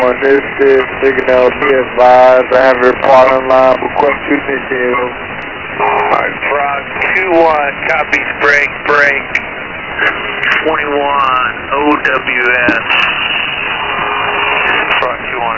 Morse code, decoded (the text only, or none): none